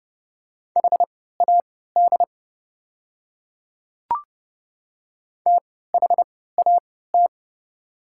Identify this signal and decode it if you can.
Morse code, heard as a keyed tone, about 30 wpm, spaced as for 17 wpm, 700 Hz. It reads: HAD THAT
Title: 3x Character Spacing - Top 200 Words - 30wpm